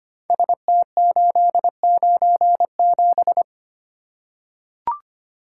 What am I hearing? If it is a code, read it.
Morse code, heard as a keyed tone, about 25 wpm, 700 Hz. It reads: ST897
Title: Code Groups - 5 Letter:Numbers - 25wpm